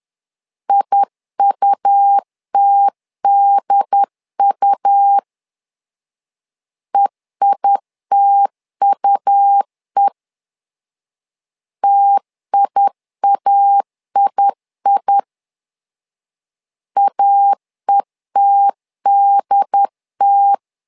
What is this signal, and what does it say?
Morse code, heard as a keyed tone, about 10 wpm, 800 Hz. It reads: IUTDU EITUE TIAII AETDT
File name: LEZIONI / 2 UHDN / 2 UHD 10wpm 1826